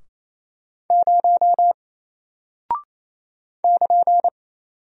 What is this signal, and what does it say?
Morse code, heard as a keyed tone, about 28 wpm, 700 Hz. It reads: 0 <KN>
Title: Single Character (Letter:Number:Punctuation:Prosign) Alt 28wpm